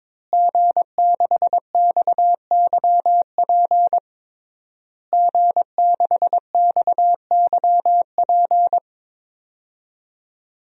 Morse code, heard as a keyed tone, about 22 wpm, 700 Hz. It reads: G6XYP G6XYP
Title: DXCC-100 Call Signs - 5 Characters - Repeated 3x - 22wpm